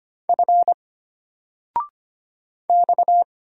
Morse code, heard as a keyed tone, about 25 wpm, 700 Hz. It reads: F X
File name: Single-Letter-Number-25wpm